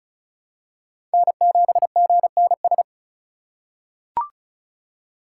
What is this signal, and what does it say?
Morse code, heard as a keyed tone, about 36 wpm, 700 Hz. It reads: N7GNS
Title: USA Call Signs - 5 Characters - 2x Character Spacing - 35wpm